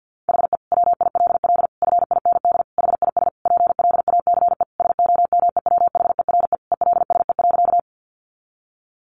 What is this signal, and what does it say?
Morse code, heard as a keyed tone, about 50 wpm, 700 Hz. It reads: HE WILL FIND HIS PLACE SOMEWHERE ELSE.
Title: Sentences from Top 1400 Words 50wpm